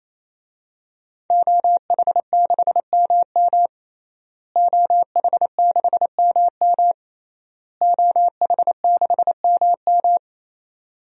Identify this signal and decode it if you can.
Morse code, heard as a keyed tone, about 28 wpm, 700 Hz. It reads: OH6MM OH6MM OH6MM
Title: DXCC-100 Call Signs - 5 Characters - Repeated 3x - 28wpm